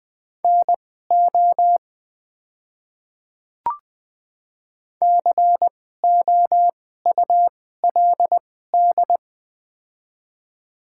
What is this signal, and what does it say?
Morse code, heard as a keyed tone, about 20 wpm, spaced as for 14 wpm, 690 Hz. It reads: NO COULD
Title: 2x Character Spacing - Top 100 Words - 20wpm